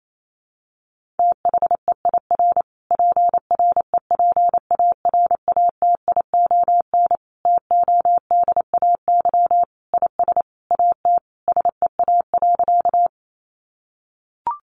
Morse code, heard as a keyed tone, about 28 wpm, 700 Hz. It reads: THEIR PREPARATION TODAY IS AT SEA.
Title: Sentences from Top 1400 Words 28wpm